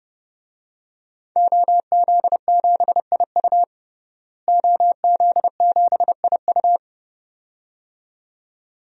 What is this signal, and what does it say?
Morse code, heard as a keyed tone, about 30 wpm, 700 Hz. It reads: OZ7IU OZ7IU
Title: DXCC-100 Call Signs - 5 Characters - Repeated 2x - 30wpm